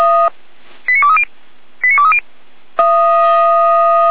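Five-tone selective call (ZVEI), heard as quick runs of five tones, tones of about 80 ms. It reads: 89230 89230
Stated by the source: ZVEI five-tone selective call 89340, received with an SDR, from the Feuerwehr Orlamünde audio server.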